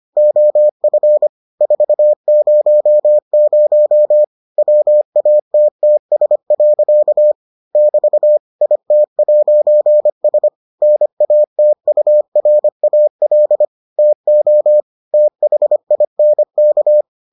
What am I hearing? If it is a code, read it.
Morse code, heard as a keyed tone, about 25 wpm, 600 Hz. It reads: OF 400 WATTS. <BT> IT'S NATURAL TO THINK